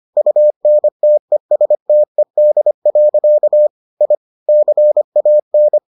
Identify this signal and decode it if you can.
Morse code, heard as a keyed tone, about 25 wpm, 600 Hz. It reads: UNTESTED. I CAN